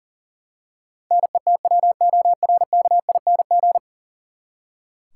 Morse code, heard as a keyed tone, about 40 wpm, 700 Hz. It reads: NETWORKING
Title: Practice QSO - Exchange 3 - More Info - 40wpm